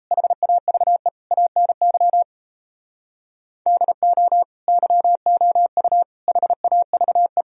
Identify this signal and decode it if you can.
Morse code, heard as a keyed tone, about 35 wpm, 700 Hz. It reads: HAVE ANY DO YOU HAVE
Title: Speed-Racing - Sets of 4 Words 28wpm